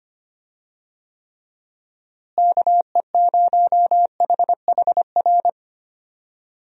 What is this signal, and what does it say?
Morse code, heard as a keyed tone, about 25 wpm, 700 Hz. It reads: KE0HHR
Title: USA Call Signs - 6 Characters - Repeated 3x - 25wpm